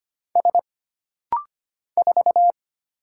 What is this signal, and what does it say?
Morse code, heard as a keyed tone, about 25 wpm, 700 Hz. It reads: S 4
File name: Single Character (Letter:Number:Punctuation:Prosign) - Rapid-Fire - 25wpm